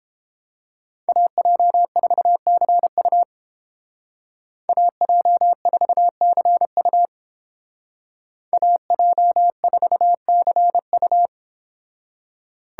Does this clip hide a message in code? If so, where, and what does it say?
Yes: Morse code from 1.1 to 11.3 s, heard as a keyed tone, about 31 wpm, 700 Hz. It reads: AJ4CU AJ4CU AJ4CU